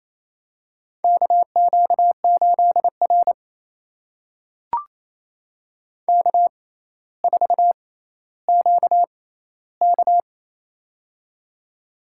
Morse code, heard as a keyed tone, about 28 wpm, spaced as for 14 wpm, 700 Hz. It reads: KQ8R K4QK
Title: USA Call Signs - 6x Character Spacing - 28wpm